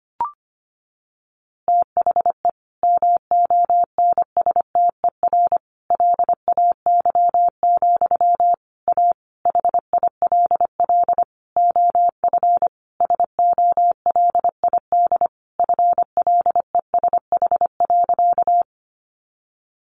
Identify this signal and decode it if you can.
Morse code, heard as a keyed tone, about 25 wpm, 700 Hz. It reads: THE MONSTER LAY, A HILL OF SOLID FLESH.
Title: A Sound Of Thunder 25wpm